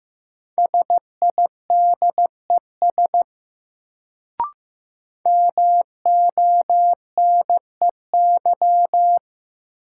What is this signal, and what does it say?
Morse code, heard as a keyed tone, about 15 wpm, 700 Hz. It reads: SIDES MONEY
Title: Sets of 4 Words - New Word Drill - 15wpm